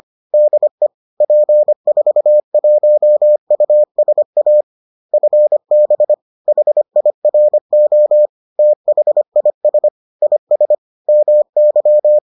Morse code, heard as a keyed tone, about 25 wpm, 600 Hz. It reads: DE P41USA FB HIRO THIS IS MY